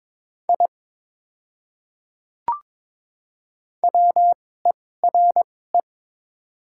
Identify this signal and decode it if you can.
Morse code, heard as a keyed tone, about 21 wpm, spaced as for 16 wpm, 700 Hz. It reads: I WERE
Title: 2x Character Spacing - Top 200 Words - 22wpm